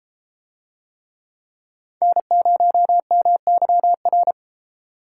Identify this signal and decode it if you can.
Morse code, heard as a keyed tone, about 32 wpm, 700 Hz. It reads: N0MYR